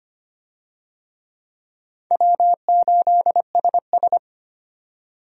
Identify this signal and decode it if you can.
Morse code, heard as a keyed tone, about 25 wpm, 700 Hz. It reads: W8SS
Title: USA Call Signs - 4 Characters - 25wpm